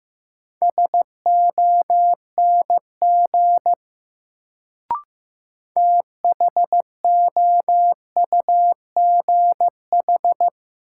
Morse code, heard as a keyed tone, about 15 wpm, 700 Hz. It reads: SONG THOUGH
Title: Sentences from Top 400 Words - New Word Drill - 15wpm